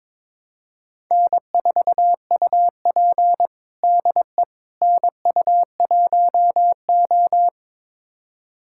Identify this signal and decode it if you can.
Morse code, heard as a keyed tone, about 22 wpm, 700 Hz. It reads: N4UP DE NU1O
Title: USA Callsign DE Callsign - 4 Characters - 22wpm